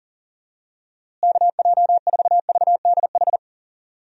Morse code, heard as a keyed tone, about 40 wpm, 700 Hz. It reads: KJ4VDH
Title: USA Call Signs - 6 Characters - Repeated 2x - 40wpm